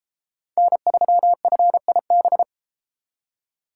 Morse code, heard as a keyed tone, about 32 wpm, 700 Hz. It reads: N3FIB